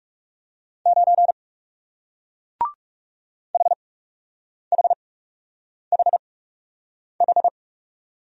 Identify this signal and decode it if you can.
Morse code, heard as a keyed tone, about 39 wpm, spaced as for 8 wpm, 700 Hz. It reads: 9 HHHH